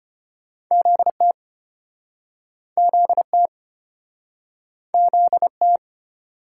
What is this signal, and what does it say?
Morse code, heard as a keyed tone, about 32 wpm, 700 Hz. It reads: ZT ZT ZT